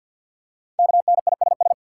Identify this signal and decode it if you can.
Morse code, heard as a keyed tone, about 52 wpm, 700 Hz. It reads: KNISS
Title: Code Groups - 5 Letters - 50wpm